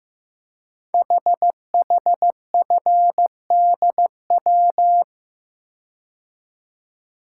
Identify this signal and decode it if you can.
Morse code, heard as a keyed tone, about 15 wpm, 690 Hz. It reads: HHFDW
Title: Code Groups - 5 Letters - 15wpm